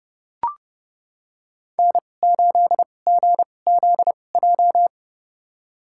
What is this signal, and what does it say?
Morse code, heard as a keyed tone, about 30 wpm, spaced as for 22 wpm, 700 Hz. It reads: N8GZJ
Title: USA Call Signs - 2x Character Spacing - 30wpm